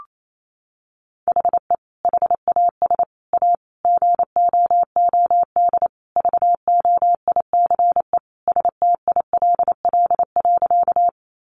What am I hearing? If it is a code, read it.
Morse code, heard as a keyed tone, about 28 wpm, 700 Hz. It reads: HE HAS A GOOD VOICE STILL.
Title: Sentences from Top 500 Words 28wpm